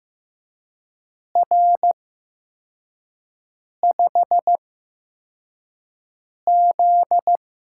Morse code, heard as a keyed tone, about 15 wpm, spaced as for 4 wpm, 700 Hz. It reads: R5Z
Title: USA Call Signs - 4 Characters - 8x Character Spacing - 15wpm